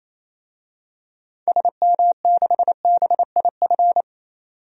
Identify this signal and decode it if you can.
Morse code, heard as a keyed tone, about 28 wpm, 700 Hz. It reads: SM6BIF
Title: DXCC-100 Call Signs - 6 Characters - 28wpm